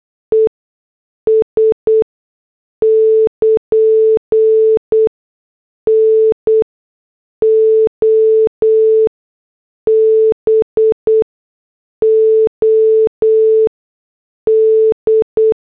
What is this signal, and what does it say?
Morse code, heard as a keyed tone, about 8 wpm, spaced as for 6 wpm, 440 Hz. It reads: ES <KN> NOBOD